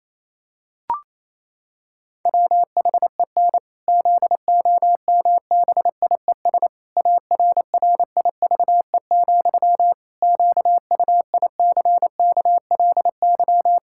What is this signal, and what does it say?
Morse code, heard as a keyed tone, about 28 wpm, 700 Hz. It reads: WHEN ZOMBIES ARRIVE, QUICKLY
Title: Pangram Challenge v2 - 28wpm